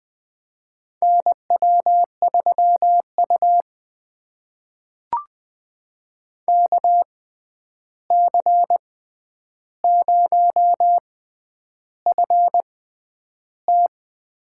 Morse code, heard as a keyed tone, about 20 wpm, spaced as for 10 wpm, 700 Hz. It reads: NW3U KC0FT
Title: USA Call Signs - 6x Character Spacing - 20wpm